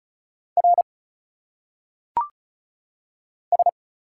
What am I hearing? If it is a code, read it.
Morse code, heard as a keyed tone, about 34 wpm, 700 Hz. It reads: R S
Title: Single Letters - Phonetic Pronunciation 35wpm